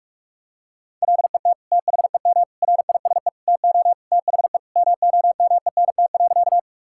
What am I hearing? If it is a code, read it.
Morse code, heard as a keyed tone, about 46 wpm, 700 Hz. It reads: LET THEM RISE TO THE MOMENT.
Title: Sentences from Top 1000 Words - Review Part 6 - 45wpm